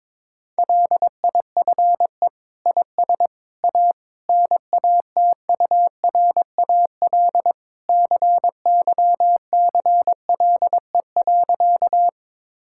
Morse code, heard as a keyed tone, about 21 wpm, 700 Hz. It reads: LIFE IS A NATURAL CYCLE.
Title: Sentences from Top 1000 Words - Review Part 6 - 22wpm